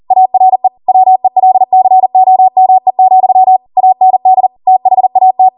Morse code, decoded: ARE WELCOME, AND THAT